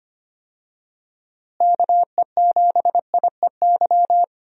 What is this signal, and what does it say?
Morse code, heard as a keyed tone, about 25 wpm, 700 Hz. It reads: KE7IEY
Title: USA Call Signs - 6 Characters - Repeated 3x - 25wpm